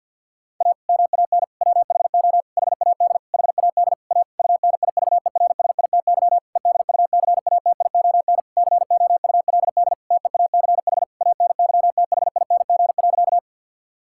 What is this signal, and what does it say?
Morse code, heard as a keyed tone, about 50 wpm, 700 Hz. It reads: A MAN WHO HAD HAD A UNIVERSITY EDUCATION COULD TEACH ANYTHING.
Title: Sentences from Top 1000 Words - Review Part 6 - 50wpm